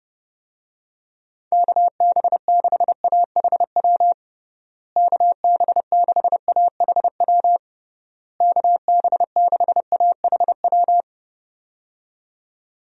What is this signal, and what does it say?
Morse code, heard as a keyed tone, about 32 wpm, 700 Hz. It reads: KB6AHW KB6AHW KB6AHW
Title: USA Call Signs - 6 Characters - Repeated 3x - 30wpm